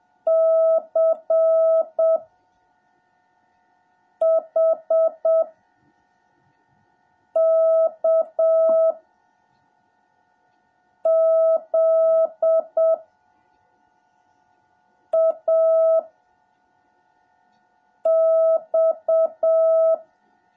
Morse code, heard as a keyed tone, about 7 wpm, 650 Hz. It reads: C H K Z A X
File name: jh8cbh / 2021cw / cw17a